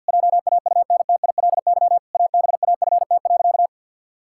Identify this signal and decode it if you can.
Morse code, heard as a keyed tone, about 51 wpm, 700 Hz. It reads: JAUNTILY ABAFT.
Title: Pangram Challenge v2 - 50wpm